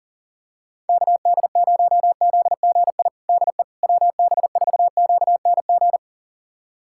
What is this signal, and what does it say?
Morse code, heard as a keyed tone, about 40 wpm, 700 Hz. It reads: KD0ZGI DE WB4QNG
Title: USA Callsign DE Callsign - 6 Characters - 40wpm